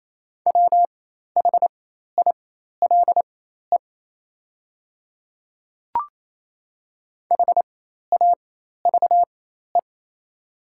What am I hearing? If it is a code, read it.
Morse code, heard as a keyed tone, about 27 wpm, spaced as for 13 wpm, 700 Hz. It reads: WHILE HAVE
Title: 4x Character Spacing - Top 300 Words - 28wpm